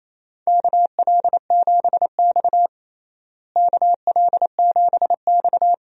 Morse code, heard as a keyed tone, about 28 wpm, 700 Hz. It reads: KL7X KL7X
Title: DXCC-100 Call Signs - 4 Characters - Repeated 2x - 28wpm